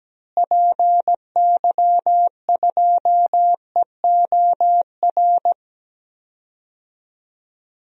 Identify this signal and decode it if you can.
Morse code, heard as a keyed tone, about 17 wpm, 700 Hz. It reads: PY2EOR
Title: DXCC-100 Call Signs - 6 Characters - Repeated 2x - 17wpm